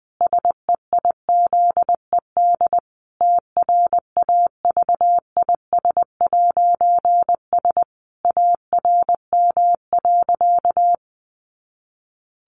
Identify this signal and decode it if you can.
Morse code, heard as a keyed tone, about 20 wpm, 700 Hz. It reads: SEIZED TRAVIS'S ARM.
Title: A Sound Of Thunder 20wpm